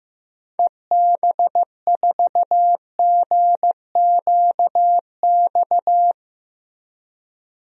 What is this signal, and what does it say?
Morse code, heard as a keyed tone, about 15 wpm, 700 Hz. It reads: EB4GQX
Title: DXCC-100 Call Signs - 6 Characters - Repeated 2x - 15wpm